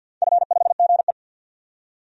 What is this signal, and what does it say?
Morse code, heard as a keyed tone, about 53 wpm, 700 Hz. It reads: F5GE
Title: DXCC-100 Call Signs - 4 Characters - 50wpm